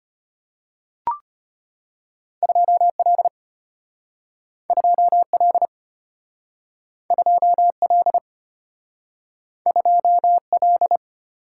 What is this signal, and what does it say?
Morse code, heard as a keyed tone, about 33 wpm, 700 Hz. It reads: 2 L 2 L 2 L 2 L